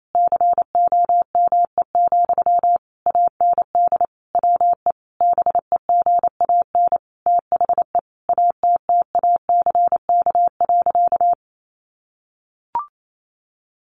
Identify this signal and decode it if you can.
Morse code, heard as a keyed tone, about 28 wpm, 700 Hz. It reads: COME, AND WE BEGAN THE ATTACK.
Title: Sentences from Top 1500 Words 28wpm